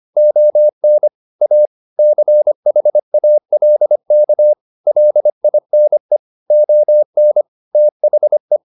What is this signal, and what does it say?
Morse code, heard as a keyed tone, about 25 wpm, 600 Hz. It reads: ON A CHALK LINE ON THE
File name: the old man and the sea part 1 25wpm